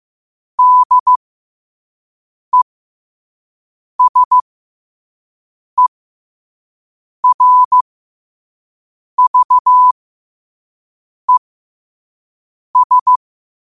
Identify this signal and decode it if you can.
Morse code, heard as a keyed tone, about 15 wpm, spaced as for 5 wpm, 1000 Hz. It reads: DESERVES
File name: qotdfiles / 5wpm